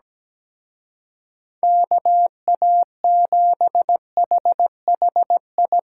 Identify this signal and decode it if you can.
Morse code, heard as a keyed tone, about 19 wpm, 700 Hz. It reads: KA7HHI
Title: Speed Racing - USA Call Signs 17wpm